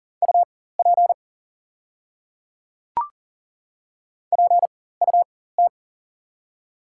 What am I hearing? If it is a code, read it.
Morse code, heard as a keyed tone, about 40 wpm, spaced as for 19 wpm, 700 Hz. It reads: UP PUT
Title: 4x Character Spacing - Top 300 Words - 40wpm